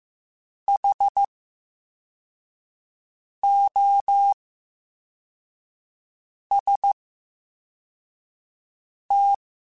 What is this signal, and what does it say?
Morse code, heard as a keyed tone, about 15 wpm, spaced as for 4 wpm, 800 Hz. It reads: HOST
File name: Special INGs / Special INGs-mp3s / Slow ing @13